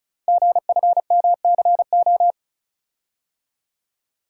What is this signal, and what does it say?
Morse code, heard as a keyed tone, about 36 wpm, 690 Hz. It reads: GFMCO
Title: Code Groups - 5 Letters - 35wpm